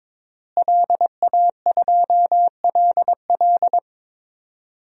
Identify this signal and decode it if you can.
Morse code, heard as a keyed tone, about 22 wpm, 700 Hz. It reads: LA2LL
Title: DXCC-100 Call Signs - 5 Characters - Repeated 2x - 22wpm